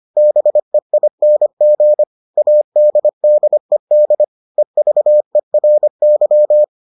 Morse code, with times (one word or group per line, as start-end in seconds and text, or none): 0.2-2.0 BEING
2.4-4.2 ADDED
4.6-6.6 EVERY